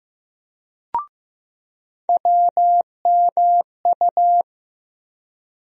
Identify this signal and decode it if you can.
Morse code, heard as a keyed tone, about 15 wpm, 700 Hz. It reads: WMU
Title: USA Call Sign Postfixes - 3 Letters 15wpm